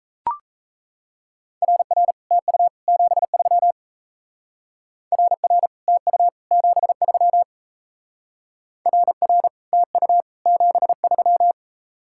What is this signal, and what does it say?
Morse code, heard as a keyed tone, about 35 wpm, 700 Hz. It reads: RR TU 73 RR TU 73 RR TU 73
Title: Speed-Racing Common QSO Phrases Encore - 28wpm